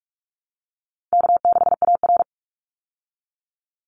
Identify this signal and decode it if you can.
Morse code, heard as a keyed tone, about 44 wpm, 700 Hz. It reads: K6AR
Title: Call Signs Repeated 2x - 45wpm